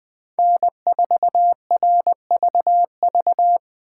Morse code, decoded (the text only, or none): N4RVV